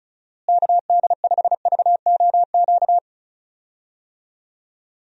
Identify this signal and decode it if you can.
Morse code, heard as a keyed tone, about 34 wpm, 700 Hz. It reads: KD5VOQ
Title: USA Call Signs - 6 Characters - Repeated 2x - 35wpm